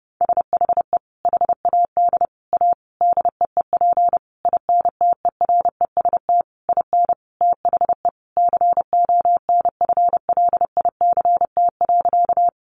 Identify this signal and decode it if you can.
Morse code, heard as a keyed tone, about 30 wpm, 700 Hz. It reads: SHE HAD A DEEP INTEREST IN THE CONFLICT.
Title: Sentences from Top 1300 Words 30wpm